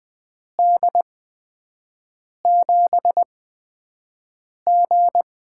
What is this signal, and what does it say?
Morse code, heard as a keyed tone, about 20 wpm, spaced as for 5 wpm, 700 Hz. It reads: D7G
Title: USA Call Signs - 8x Character Spacing - 20wpm